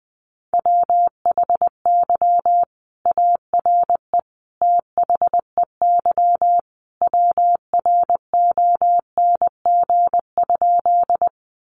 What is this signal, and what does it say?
Morse code, heard as a keyed tone, about 20 wpm, 700 Hz. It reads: WHY ARE THEY WRONG?
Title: Sentences from Top 900 Words 20wpm